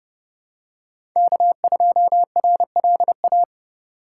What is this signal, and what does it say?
Morse code, heard as a keyed tone, about 30 wpm, 700 Hz. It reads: K2RLA
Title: Code Groups - 5 Letter:Numbers - 30wpm